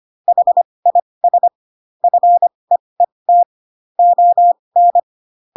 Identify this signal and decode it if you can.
Morse code, heard as a keyed tone, about 25 wpm, spaced as for 20 wpm, 700 Hz. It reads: HIS FEET ON